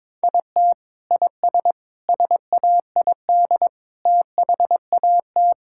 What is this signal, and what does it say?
Morse code, heard as a keyed tone, about 22 wpm, 700 Hz. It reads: IT IS SAID THAT
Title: Sentences from Top 1000 Words - Review Part 6 - 22wpm